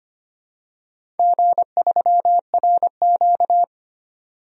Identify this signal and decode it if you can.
Morse code, heard as a keyed tone, about 25 wpm, 700 Hz. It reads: G3RQ
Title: DXCC-100 Call Signs - 4 Characters - 25wpm